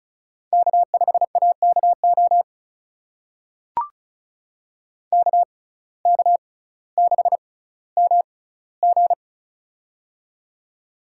Morse code, heard as a keyed tone, about 36 wpm, spaced as for 18 wpm, 700 Hz. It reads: K5AKO KK6MG
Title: USA Call Signs - 5 Characters - 6x Character Spacing - 35wpm